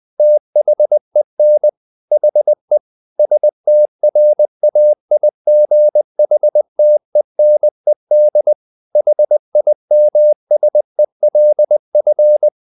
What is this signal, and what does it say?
Morse code, heard as a keyed tone, about 20 wpm, 600 Hz. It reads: THEN HE STRAIGHTENED HIMSELF
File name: the old man and the sea part 2 20wpm